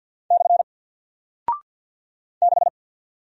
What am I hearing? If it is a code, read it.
Morse code, heard as a keyed tone, about 50 wpm, 700 Hz. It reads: / 6
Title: Single Character (Letter:Number:Punctuation:Prosign) Alt 50wpm